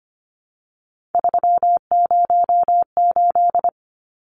Morse code, heard as a keyed tone, about 25 wpm, 700 Hz. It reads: 308
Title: Numbers - 3-Digit 25wpm